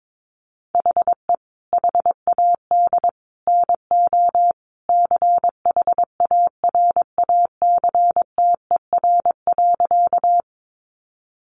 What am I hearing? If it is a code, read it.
Morse code, heard as a keyed tone, about 22 wpm, 700 Hz. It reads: HE HAD NO CHARACTER.
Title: Sentences from Top 1200 Words 22wpm